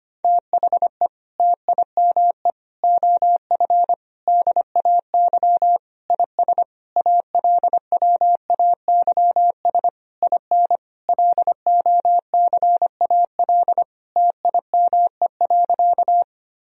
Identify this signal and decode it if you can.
Morse code, heard as a keyed tone, about 25 wpm, 700 Hz. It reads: THE TIME OF DAY IS ALWAYS IN LOCAL TIME.
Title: Sentences from Top 1000 Words - Review Part 3 - 25wpm